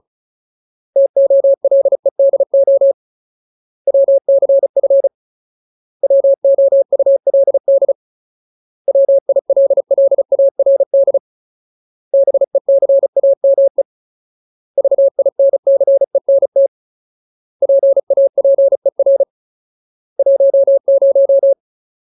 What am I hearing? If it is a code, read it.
Morse code, heard as a keyed tone, about 35 wpm, 550 Hz. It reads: TOLEDO WCF WOULD WILLARD BECAME VINCENT PAPER 10
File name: Ultimate Word List / 35WPM Ultimate